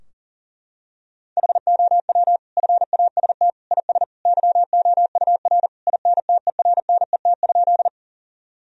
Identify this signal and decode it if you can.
Morse code, heard as a keyed tone, about 40 wpm, 700 Hz. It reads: HOW FAST IS YOUR INTERNET?
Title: Sentences from Top 1000 Words - Review Part 4 - 40wpm